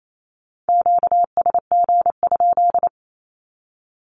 Morse code, heard as a keyed tone, about 28 wpm, 700 Hz. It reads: QSG?
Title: Q-Codes 28wpm